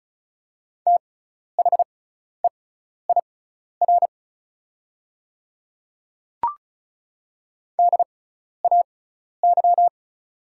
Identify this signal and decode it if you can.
Morse code, heard as a keyed tone, about 36 wpm, spaced as for 12 wpm, 700 Hz. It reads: THEIR DAY
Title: 6x Character Spacing - Top 100 Words - 35wpm